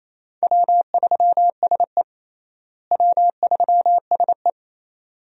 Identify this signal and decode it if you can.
Morse code, heard as a keyed tone, about 27 wpm, 700 Hz. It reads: W3SE W3SE
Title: USA Call Signs - 4 Characters - Repeated 2x - 28wpm